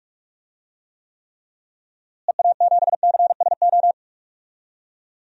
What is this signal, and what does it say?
Morse code, heard as a keyed tone, about 46 wpm, 700 Hz. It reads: EA7CSO
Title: DXCC-100 Call Signs - 6 Characters - Repeated 3x - 45wpm